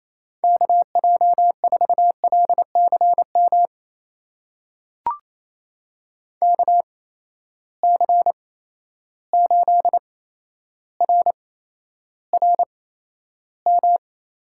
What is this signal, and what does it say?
Morse code, heard as a keyed tone, about 28 wpm, spaced as for 12 wpm, 700 Hz. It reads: KJ4LCM KC8RRM
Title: USA Call Signs - 6 Characters - 8x Character Spacing - 28wpm